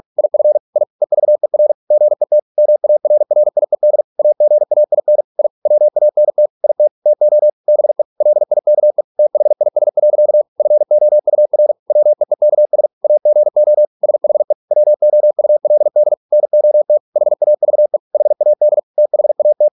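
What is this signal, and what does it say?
Morse code, heard as a keyed tone, about 48 wpm, 600 Hz. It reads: IF I EVER GET MARRIED AGAIN I WANT IT TO BE LIKE THIS. FOUR WEEKS AGO SHE WOULD NOT HAVE HAD THAT
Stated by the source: Morse practice at 45 wpm, the novel Never by Ken Follett in English, from CW med Gnister.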